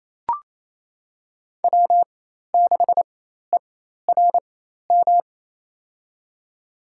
Morse code, heard as a keyed tone, about 28 wpm, spaced as for 13 wpm, 700 Hz. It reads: W6ERM
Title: USA Call Signs - 5 Characters - 4x Character Spacing - 28wpm